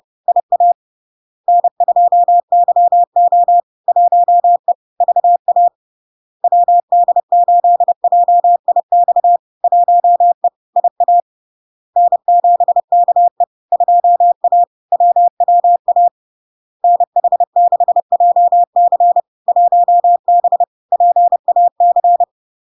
Morse code, heard as a keyed tone, about 30 wpm, 700 Hz. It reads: IA N2YO 1E VA WD8JIX 1E IA N7KE 2A WWA NH6JC 1B PAC
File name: LARK-FD-2022-Log-in-CW-30wpm-700-Hz